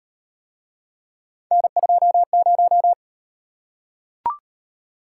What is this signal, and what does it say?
Morse code, heard as a keyed tone, about 36 wpm, 700 Hz. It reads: N20